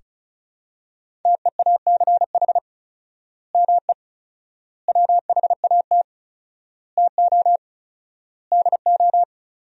Morse code, heard as a keyed tone, about 34 wpm, 700 Hz. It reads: TEACH ME WHAT TO DO
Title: 4x Word-Spacing - Sentences from Top 500 Words - 35wpm